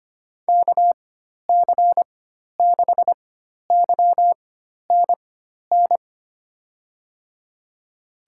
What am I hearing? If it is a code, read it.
Morse code, heard as a keyed tone, about 25 wpm, spaced as for 12 wpm, 700 Hz. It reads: KC6YNN
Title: USA Call Signs - 4x Character Spacing - 25wpm